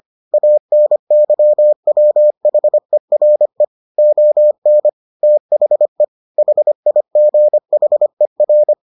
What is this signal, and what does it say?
Morse code, heard as a keyed tone, about 25 wpm, 600 Hz. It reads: ANYWHERE ON THE HIGHER